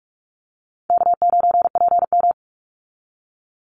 Morse code, heard as a keyed tone, about 44 wpm, 700 Hz. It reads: K9PM
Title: NA SOTA Chaser Callsigns 45wpm